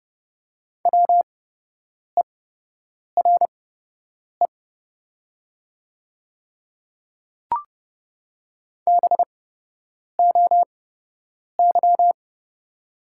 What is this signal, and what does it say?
Morse code, heard as a keyed tone, about 30 wpm, spaced as for 8 wpm, 700 Hz. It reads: WERE BOY
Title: 8x Character Spacing - Top 200 Words - 30wpm